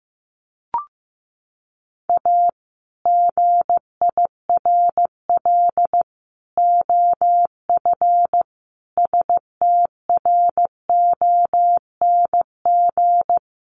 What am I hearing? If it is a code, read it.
Morse code, heard as a keyed tone, about 14 wpm, 700 Hz. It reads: A GIRL OF STRONG
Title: Sentences from Top 1200 Words 15wpm